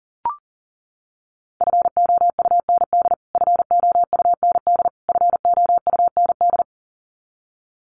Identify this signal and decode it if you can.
Morse code, heard as a keyed tone, about 40 wpm, 700 Hz. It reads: FOUND FOUND FOUND
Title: Top 500 Words Repeated 40wpm